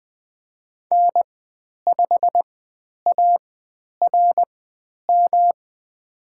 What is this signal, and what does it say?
Morse code, heard as a keyed tone, about 20 wpm, spaced as for 10 wpm, 700 Hz. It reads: N5ARM
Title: 109 Lesson 18g - Letter m - Call Signs 20wpm